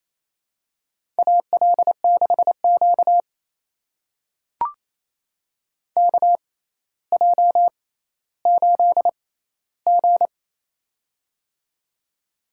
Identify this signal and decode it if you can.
Morse code, heard as a keyed tone, about 28 wpm, spaced as for 14 wpm, 700 Hz. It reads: AL6Q KJ8G
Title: USA Call Signs - 4 Characters - 6x Character Spacing - 28wpm